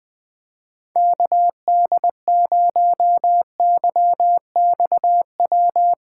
Morse code, heard as a keyed tone, about 20 wpm, 700 Hz. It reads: KD0YXW
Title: Speed Racing - USA 2x3 Call Signs 17wpm